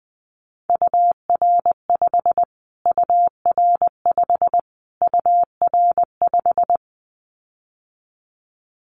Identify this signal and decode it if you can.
Morse code, heard as a keyed tone, about 20 wpm, 700 Hz. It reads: UR5 UR5 UR5
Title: Callsign Prefixes for Top 25 DXCC - Repeated 3x - 20wpm